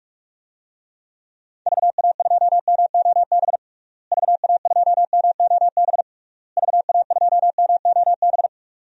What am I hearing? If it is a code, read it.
Morse code, heard as a keyed tone, about 41 wpm, 700 Hz. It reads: VA2MOB VA2MOB VA2MOB
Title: DXCC-100 Call Signs - 6 Characters - Repeated 3x - 45wpm